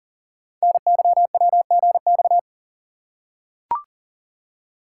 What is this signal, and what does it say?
Morse code, heard as a keyed tone, about 39 wpm, 690 Hz. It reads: NYWGX
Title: Code Groups - 5 Letter:Numbers - 40wpm